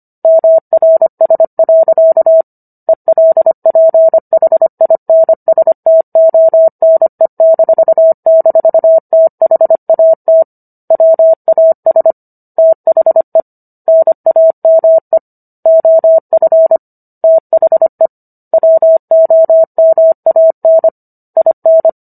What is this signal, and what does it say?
Morse code, heard as a keyed tone, about 25 wpm, 650 Hz. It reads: MRS. ELPHINSTONE--THAT WAS THE NAME OF THE WOMAN IN